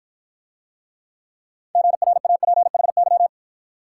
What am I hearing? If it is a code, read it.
Morse code, heard as a keyed tone, about 53 wpm, 700 Hz. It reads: GRAPHY